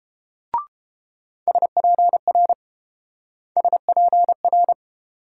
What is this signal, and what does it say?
Morse code, heard as a keyed tone, about 30 wpm, 700 Hz. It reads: SPR SPR